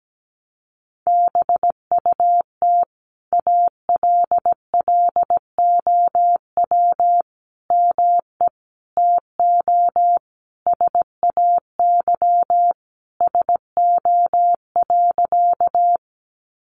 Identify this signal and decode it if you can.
Morse code, heard as a keyed tone, about 17 wpm, 700 Hz. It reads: BUT ALLOW ME TO SAY SO.
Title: Sentences from Top 700 Words 17wpm